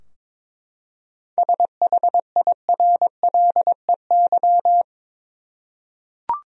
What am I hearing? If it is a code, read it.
Morse code, heard as a keyed tone, about 22 wpm, 700 Hz. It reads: SHIRLEY
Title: USA Common Names - 22wpm